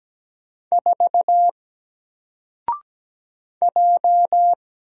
Morse code, heard as a keyed tone, about 17 wpm, 700 Hz. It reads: 4 J
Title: Single Letter-Number 17wpm